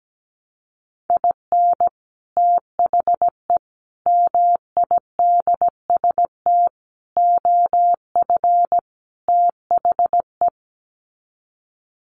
Morse code, heard as a keyed tone, about 17 wpm, 700 Hz. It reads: IN THE MIDST OF THE
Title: Sets of 5 Words 17wpm